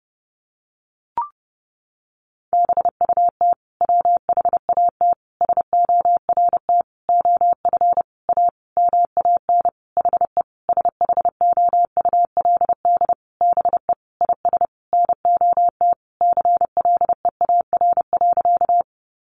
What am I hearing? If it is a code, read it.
Morse code, heard as a keyed tone, about 30 wpm, 700 Hz. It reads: BUT WHAT SORT OF A MAN HE SHOULD BE IS NOT CLEAR.
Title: Sentences from Top 1300 Words 30wpm